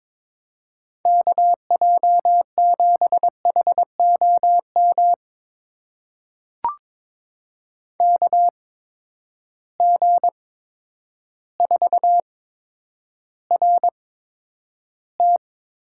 Morse code, heard as a keyed tone, about 22 wpm, spaced as for 10 wpm, 700 Hz. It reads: KJ7HOM KG4RT
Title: USA Call Signs - 6 Characters - 8x Character Spacing - 22wpm